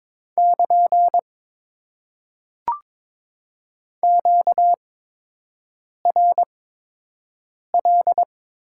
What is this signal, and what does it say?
Morse code, heard as a keyed tone, about 22 wpm, spaced as for 6 wpm, 700 Hz. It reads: <KN> QRL
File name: Top 90 QSO Elements - 8x Character Spacing - 22wpm